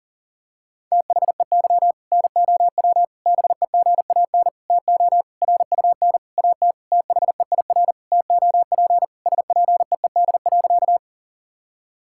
Morse code, heard as a keyed tone, about 40 wpm, 700 Hz. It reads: THEY NOW BEGAN TO RUN AT THEIR TOP SPEED.
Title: Sentences from Top 1000 Words - Review Part 5 - 40wpm